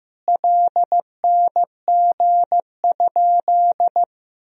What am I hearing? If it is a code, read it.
Morse code, heard as a keyed tone, about 16 wpm, 700 Hz. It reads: LNG?
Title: Sentences from Top 200 Words and Abbreviations 15wpm